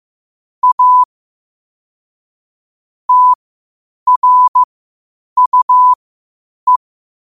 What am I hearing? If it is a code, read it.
Morse code, heard as a keyed tone, about 15 wpm, spaced as for 8 wpm, 1000 Hz. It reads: A TRUE
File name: qotdfiles / 8wpm